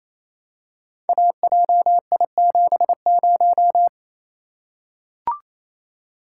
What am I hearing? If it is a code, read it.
Morse code, heard as a keyed tone, about 28 wpm, 700 Hz. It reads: AJI70
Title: Code Groups - 5 Letter:Numbers - 28wpm